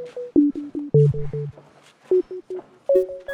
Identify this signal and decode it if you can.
There is no code.